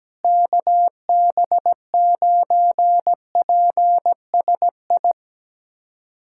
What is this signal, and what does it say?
Morse code, heard as a keyed tone, about 17 wpm, 700 Hz. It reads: KB9PSI
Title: USA Callsign DE Callsign - 6 Characters - 17wpm